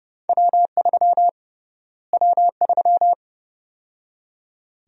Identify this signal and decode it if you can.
Morse code, heard as a keyed tone, about 30 wpm, 700 Hz. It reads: W3 W3
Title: USA Call Sign Prefixes - Repeated 2x - 30wpm